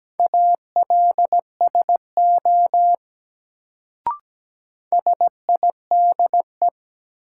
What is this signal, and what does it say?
Morse code, heard as a keyed tone, about 17 wpm, 700 Hz. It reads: ALSO SIDE
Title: Sentences from Top 400 Words - New Word Drill - 17wpm